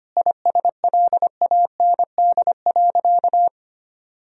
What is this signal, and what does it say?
Morse code, heard as a keyed tone, about 25 wpm, 700 Hz. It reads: ISLAND.
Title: Sentences from Top 1000 Words - Review Part 3 - 25wpm